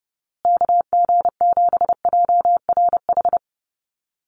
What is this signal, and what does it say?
Morse code, heard as a keyed tone, about 30 wpm, 700 Hz. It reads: KG7JRH
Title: USA Call Signs Repeated 2x 30wpm